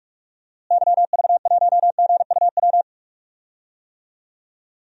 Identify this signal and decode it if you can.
Morse code, heard as a keyed tone, about 46 wpm, 700 Hz. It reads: YV1GUW
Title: DXCC-100 Call Signs - 6 Characters - Repeated 3x - 45wpm